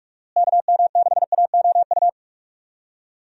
Morse code, heard as a keyed tone, about 44 wpm, 700 Hz. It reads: KM6AOU
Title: USA Call Signs - 6 Characters - Repeated 3x - 45wpm